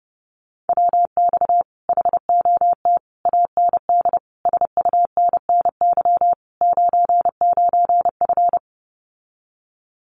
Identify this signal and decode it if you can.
Morse code, heard as a keyed tone, about 30 wpm, 700 Hz. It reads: WX HOT AND SUNNY 99F
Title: Common QSO Phrases Encore 30wpm